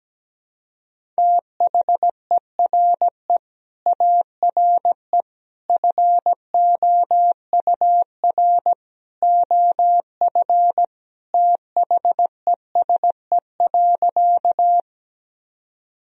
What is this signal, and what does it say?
Morse code, heard as a keyed tone, about 17 wpm, 700 Hz. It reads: THERE ARE FOUR OF THESE.
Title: Sentences from Top 1000 Words - Review Part 1 - 17wpm